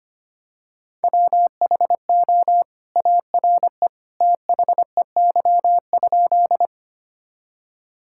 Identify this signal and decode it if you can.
Morse code, heard as a keyed tone, about 25 wpm, 700 Hz. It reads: WHO ARE THEY?
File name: practice / 25wpm / Sentences with 2-Word Phrases - 25wpm